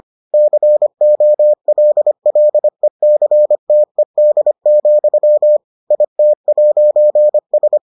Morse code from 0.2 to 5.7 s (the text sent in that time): COLLECTED,